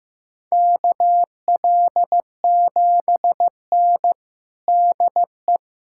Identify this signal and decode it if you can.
Morse code, heard as a keyed tone, about 15 wpm, 700 Hz. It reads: KL7N DE
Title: USA Callsign DE Callsign - 4 Characters - 15wpm